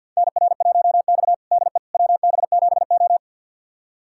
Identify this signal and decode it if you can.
Morse code, heard as a keyed tone, about 52 wpm, 700 Hz. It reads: NR1X DE WB7O
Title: USA Callsign DE Callsign - 4 Characters - 50wpm